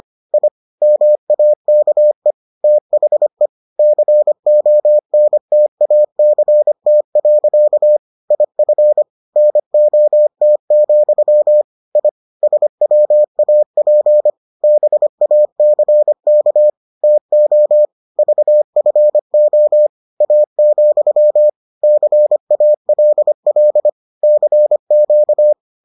Morse code, heard as a keyed tone, about 25 wpm, 600 Hz. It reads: I MAKE THE CONTACT. IF NOT, I SWAP BACK TO VFO A, CALL CQ